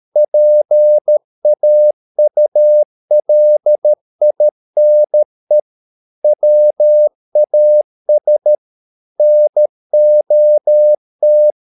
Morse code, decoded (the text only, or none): PAULINE WAS NOT